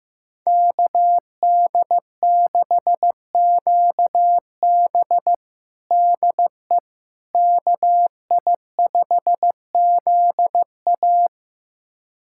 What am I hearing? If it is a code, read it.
Morse code, heard as a keyed tone, about 15 wpm, 700 Hz. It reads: KD6QB DE KI5ZA
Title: USA Callsign DE Callsign - 5 Characters - 15wpm